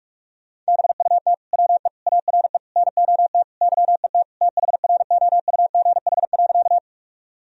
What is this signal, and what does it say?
Morse code, heard as a keyed tone, about 46 wpm, 700 Hz. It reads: BUT WE ARE NOT YET THROUGH.
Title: Sentences from Top 1000 Words - Review Part 1 - 45wpm